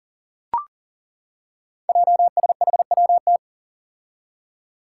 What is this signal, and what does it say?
Morse code, heard as a keyed tone, about 38 wpm, 700 Hz. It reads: JSHWT
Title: Five Challenging Letter-Number Combinations 40wpm